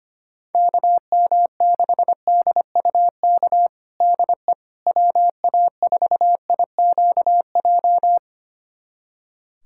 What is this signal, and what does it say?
Morse code, heard as a keyed tone, about 25 wpm, 700 Hz. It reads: KM6DUK DE WA4IQJ
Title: USA Callsign DE Callsign - 6 Characters - 25wpm